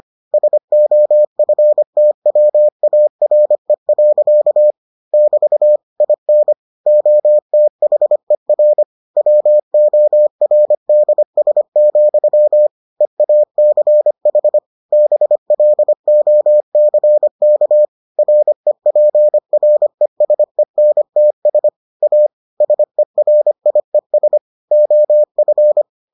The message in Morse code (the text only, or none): SOFTWARE. <BT> IN OTHER WORDS, EACH BLOCK REPRESENTS A SERIES OF